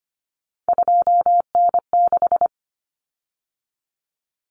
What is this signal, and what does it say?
Morse code, heard as a keyed tone, about 25 wpm, 700 Hz. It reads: 2N6
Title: Common Cut Numbers - 3-Digit 25wpm